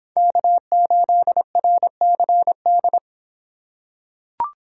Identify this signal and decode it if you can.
Morse code, heard as a keyed tone, about 26 wpm, 700 Hz. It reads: K8RCD